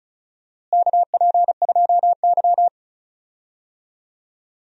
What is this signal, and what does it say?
Morse code, heard as a keyed tone, about 36 wpm, 700 Hz. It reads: KP2Y
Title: DXCC-100 Call Signs - 4 Characters - Repeated 2x - 35wpm